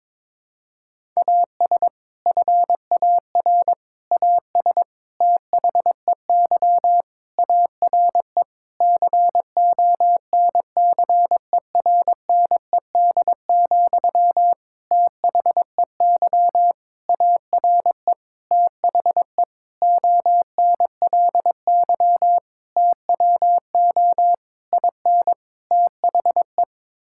Morse code, heard as a keyed tone, about 22 wpm, 700 Hz. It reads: AS FAR AS THEY ARE CONCERNED, THEY ARE THE ONLY TWO IN THE
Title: Sentences from Top 1000 Words - Review Part 6 - 22wpm